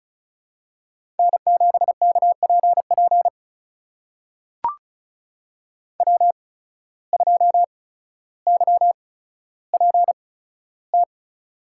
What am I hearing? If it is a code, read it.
Morse code, heard as a keyed tone, about 36 wpm, spaced as for 15 wpm, 700 Hz. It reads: N7KPP W2YPT
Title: USA Call Signs - 5 Characters - 8x Character Spacing - 35wpm